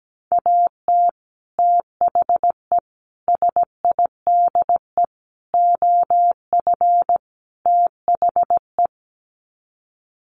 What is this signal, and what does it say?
Morse code, heard as a keyed tone, about 17 wpm, 700 Hz. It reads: AT THE SIDE OF THE
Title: Sets of 5 Words Encore2 17wpm